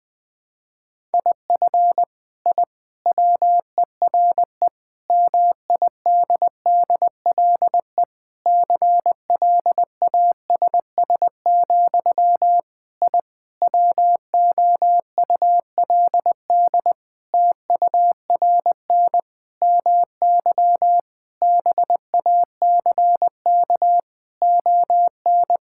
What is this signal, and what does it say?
Morse code, heard as a keyed tone, about 20 wpm, 700 Hz. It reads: IF I WERE MIDDLE CLASS, I WOULD TURN MY BACK ON
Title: Sentences from Top 1000 Words - Review Part 6 - 20wpm